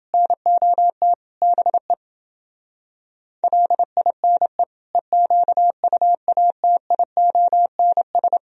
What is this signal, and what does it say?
Morse code, heard as a keyed tone, about 28 wpm, 700 Hz. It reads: NOT BE LINE EQUATIONS